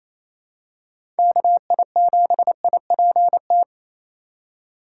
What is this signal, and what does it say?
Morse code, heard as a keyed tone, about 28 wpm, 700 Hz. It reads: KI7IPT
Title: USA Call Signs - 6 Characters - 4x Character Spacing - 28wpm